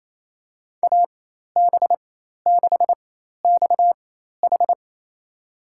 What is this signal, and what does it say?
Morse code, heard as a keyed tone, about 27 wpm, spaced as for 13 wpm, 700 Hz. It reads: AB6XH
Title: USA Call Signs - 5 Characters - 4x Character Spacing - 28wpm